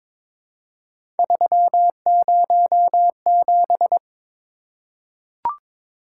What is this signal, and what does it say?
Morse code, heard as a keyed tone, about 22 wpm, 700 Hz. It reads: 307
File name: practice / 22wpm / Three Letter-Number Combinations - 22wpm